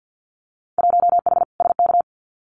Morse code, heard as a keyed tone, about 50 wpm, 700 Hz. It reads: 1H SK
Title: Winter Field Day Exchanges 50wpm